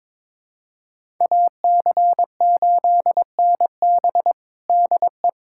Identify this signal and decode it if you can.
Morse code, heard as a keyed tone, about 22 wpm, 700 Hz. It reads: AC8NB DE
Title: USA Callsign DE Callsign 22wpm